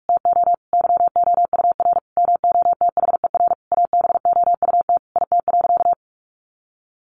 Morse code, heard as a keyed tone, about 44 wpm, 700 Hz. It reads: TO YOUR MOTHER ABOUT IT.
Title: Sentences from Top 1100 Words 45wpm